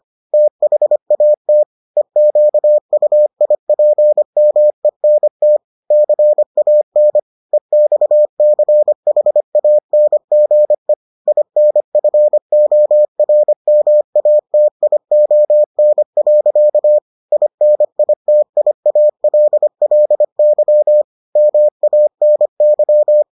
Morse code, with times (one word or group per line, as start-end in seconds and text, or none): 0.3-1.6 THAT
2.0-5.6 EQUIPMENT
5.9-7.2 CAN
7.5-10.9 EXCHANGE
11.3-17.0 INFORMATION.
17.3-21.0 INITIALLY
21.4-23.2 MANY